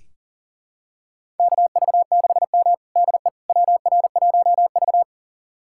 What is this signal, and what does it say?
Morse code, heard as a keyed tone, about 37 wpm, 700 Hz. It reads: KV6M DE WR1V
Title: USA Callsign DE Callsign - 4 Characters - 40wpm